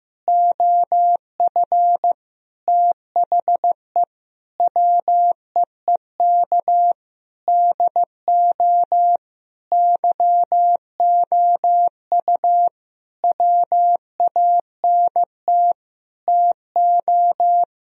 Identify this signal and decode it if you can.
Morse code, heard as a keyed tone, about 15 wpm, 700 Hz. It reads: OF THE WEEK DO YOU WANT TO
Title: Sentences from Top 1000 Words - Review Part 3 - 15wpm